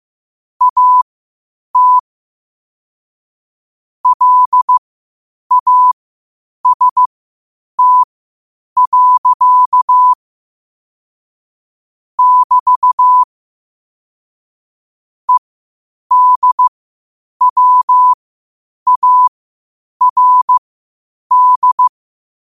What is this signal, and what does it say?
Morse code, heard as a keyed tone, about 15 wpm, spaced as for 8 wpm, 1000 Hz. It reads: AT LAST. <BT> EDWARD
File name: qotdfiles / 8wpm